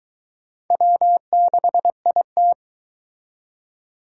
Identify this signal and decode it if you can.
Morse code, heard as a keyed tone, about 22 wpm, 700 Hz. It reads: W6IT